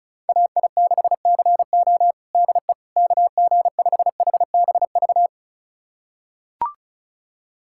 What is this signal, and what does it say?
Morse code, heard as a keyed tone, about 36 wpm, 700 Hz. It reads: AI6CO DE KG5HBV
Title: USA Callsign DE Callsign 35wpm